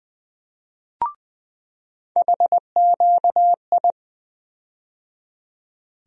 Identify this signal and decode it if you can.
Morse code, heard as a keyed tone, about 20 wpm, 700 Hz. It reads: HQI
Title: USA Call Sign Postfixes - 3 Letters 20wpm